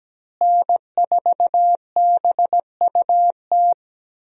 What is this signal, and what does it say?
Morse code, heard as a keyed tone, about 17 wpm, 700 Hz. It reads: N4BUT
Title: USA Call Signs - 5 Characters - Repeated 3x - 17wpm